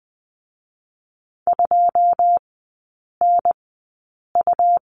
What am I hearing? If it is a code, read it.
Morse code, heard as a keyed tone, about 20 wpm, spaced as for 8 wpm, 700 Hz. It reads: 2NU
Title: 127 Lesson 21e - Number 6 - Three Characters 20wpm